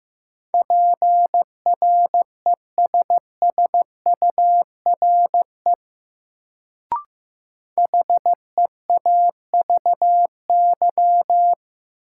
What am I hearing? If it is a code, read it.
Morse code, heard as a keyed tone, about 15 wpm, 700 Hz. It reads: PRESSURE HEAVY
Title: Sentences from Top 1000 Words - Review Part 6 - New Word Drill - 15wpm